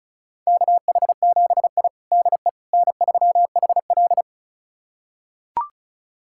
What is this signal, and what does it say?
Morse code, heard as a keyed tone, about 36 wpm, 700 Hz. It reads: KH7I DE N3HL
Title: USA Callsign DE Callsign - 4 Characters - 35wpm